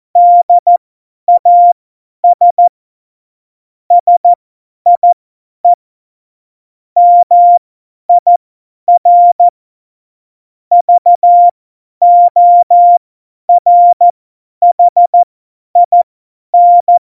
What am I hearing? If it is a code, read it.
Morse code, heard as a keyed tone, about 14 wpm, spaced as for 10 wpm, 700 Hz. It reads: DAS SIE MIR VORHIN
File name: Morsekurs / Witze / Witz 13. Oktober 2020, 50 F70